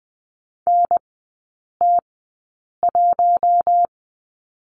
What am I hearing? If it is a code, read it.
Morse code, heard as a keyed tone, about 20 wpm, spaced as for 8 wpm, 700 Hz. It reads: NT1
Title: 041 Lesson 8e - Number 1 - Three Characters 20wpm